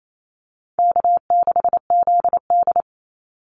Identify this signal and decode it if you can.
Morse code, heard as a keyed tone, about 28 wpm, 700 Hz. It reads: K6ZD
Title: Easy Call Signs - 28wpm